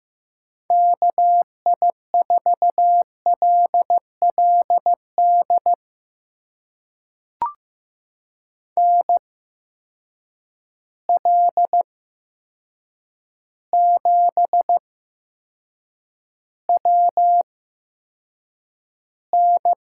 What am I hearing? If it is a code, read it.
Morse code, heard as a keyed tone, about 15 wpm, spaced as for 7 wpm, 700 Hz. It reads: KI4LLD NL7WN
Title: USA Call Signs - 8x Character Spacing - 15wpm